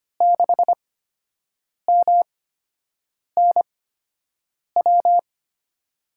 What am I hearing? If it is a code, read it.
Morse code, heard as a keyed tone, about 25 wpm, spaced as for 7 wpm, 700 Hz. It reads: 6MNW
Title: USA Call Signs - 6 Characters - 8x Character Spacing - 25wpm